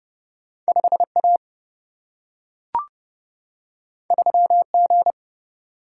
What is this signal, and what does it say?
Morse code, heard as a keyed tone, about 30 wpm, 700 Hz. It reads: 5 A 3 G
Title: Single-Letter Single-Number 30wpm